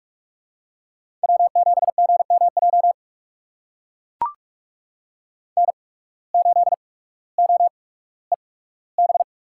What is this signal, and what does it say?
Morse code, heard as a keyed tone, about 44 wpm, spaced as for 19 wpm, 700 Hz. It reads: W7GMJ N8OEB
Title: USA Call Signs - 5 Characters - 8x Character Spacing - 45wpm